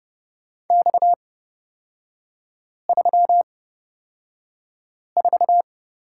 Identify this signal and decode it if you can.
Morse code, heard as a keyed tone, about 30 wpm, spaced as for 5 wpm, 700 Hz. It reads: X34
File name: Lesson 37e - Letter x - Three Characters 30wpm